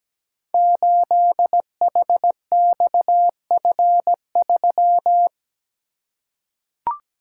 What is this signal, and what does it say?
Morse code, heard as a keyed tone, about 17 wpm, 700 Hz. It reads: 8HXF3
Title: Code Groups - 5 Letter:Numbers - 17wpm